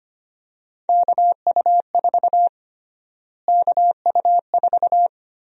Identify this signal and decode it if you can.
Morse code, heard as a keyed tone, about 25 wpm, 700 Hz. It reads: KU4 KU4
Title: USA Call Sign Prefixes - Repeated 2x - 25wpm